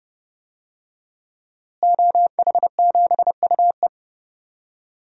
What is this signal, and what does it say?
Morse code, heard as a keyed tone, about 30 wpm, 700 Hz. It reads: OH7UE
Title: DXCC-100 Call Signs - 5 Characters - Repeated 3x - 30wpm